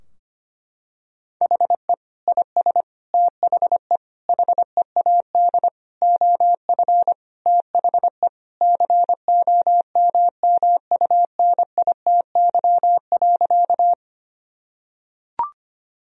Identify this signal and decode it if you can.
Morse code, heard as a keyed tone, about 25 wpm, 700 Hz. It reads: HE IS THE HEAD OF THE COMMUNITY.
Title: Sentences from Top 1000 Words - Review Part 4 - 25wpm